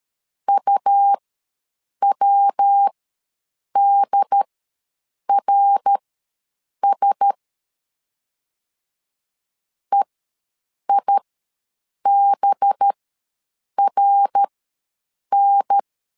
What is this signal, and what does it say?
Morse code, heard as a keyed tone, about 13 wpm, spaced as for 7 wpm, 800 Hz. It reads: UWDRS EIBRN
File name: 5 LB ws 12wpm